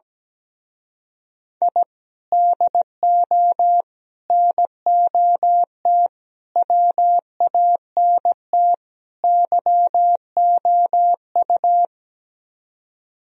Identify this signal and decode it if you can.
Morse code, heard as a keyed tone, about 17 wpm, 700 Hz. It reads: I DO NOT WANT YOU
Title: Speed-Racing - Sets of 5 Words 17wpm